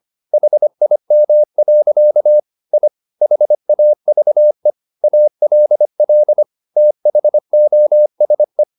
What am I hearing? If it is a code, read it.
Morse code, heard as a keyed tone, about 25 wpm, 600 Hz. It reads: HIM. I HAVE ALL THOSE